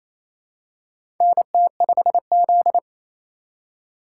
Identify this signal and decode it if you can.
Morse code, heard as a keyed tone, about 28 wpm, 700 Hz. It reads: NT5Z
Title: USA Call Signs - 4 Characters - 2x Character Spacing - 28wpm